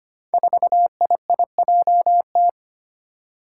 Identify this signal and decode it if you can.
Morse code, heard as a keyed tone, about 25 wpm, 700 Hz. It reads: 4IIJT
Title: Five Challenging Letter-Number Combinations 25wpm